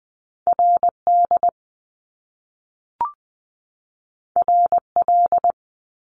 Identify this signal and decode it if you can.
Morse code, heard as a keyed tone, about 20 wpm, 700 Hz. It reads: RD RL
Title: Top Two Letter Combinations in Words 20wpm